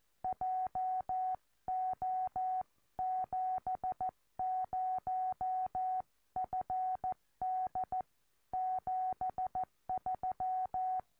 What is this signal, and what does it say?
Morse code, heard as a keyed tone, about 13 wpm, 740 Hz. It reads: JO70FD 73
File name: tropo / gm4wja